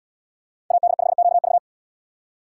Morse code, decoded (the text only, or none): AD5QV